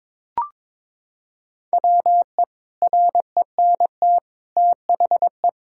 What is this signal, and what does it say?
Morse code, heard as a keyed tone, about 22 wpm, 700 Hz. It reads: WE RENT THE
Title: Sentences from Top 1000 Words - Review Part 5 - 22wpm